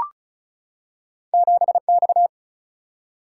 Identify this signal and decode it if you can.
Morse code, heard as a keyed tone, about 36 wpm, 700 Hz. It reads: 7 X
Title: Single-Letter Single-Number 35wpm